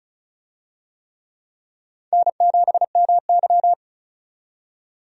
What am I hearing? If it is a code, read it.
Morse code, heard as a keyed tone, about 36 wpm, 700 Hz. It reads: N7MY
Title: USA Call Signs - 4 Characters - 35wpm